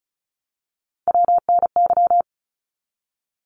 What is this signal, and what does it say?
Morse code, heard as a keyed tone, about 34 wpm, 700 Hz. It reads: WNY
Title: ARRL Sections 35wpm